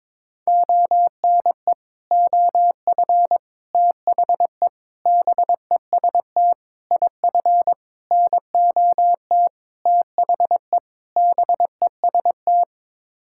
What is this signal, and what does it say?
Morse code, heard as a keyed tone, about 21 wpm, 700 Hz. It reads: ONE OF THE BEST IF NOT THE BEST
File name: practice / 22wpm / Sets of 8 Words - 22wpm